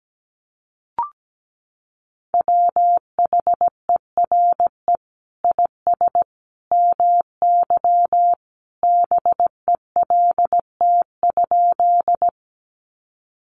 Morse code, heard as a keyed tone, about 17 wpm, 700 Hz. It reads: WHERE IS MY BELT?
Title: Sentences from Top 1400 Words 17wpm